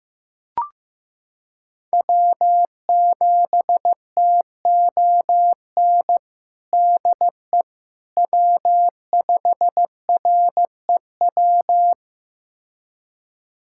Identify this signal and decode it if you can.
Morse code, heard as a keyed tone, about 15 wpm, 700 Hz. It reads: W7TON DE W5REW
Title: USA Callsign DE Callsign - 5 Characters - 15wpm